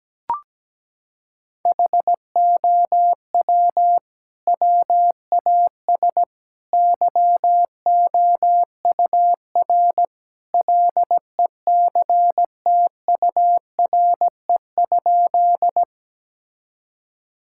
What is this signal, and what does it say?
Morse code, heard as a keyed tone, about 17 wpm, 700 Hz. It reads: HOW WAS YOUR LECTURE?
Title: Sentences from Top 1600 Words 17wpm